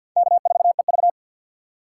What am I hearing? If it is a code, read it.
Morse code, heard as a keyed tone, about 53 wpm, 700 Hz. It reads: K4EV
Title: USA Call Signs - 4 Characters - 3x Character Spacing - 50wpm